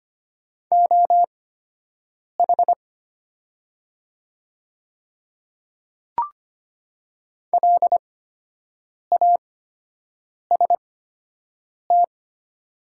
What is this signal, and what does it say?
Morse code, heard as a keyed tone, about 25 wpm, spaced as for 7 wpm, 690 Hz. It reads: OH LAST
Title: 8x Character Spacing - Top 500 Words - 25wpm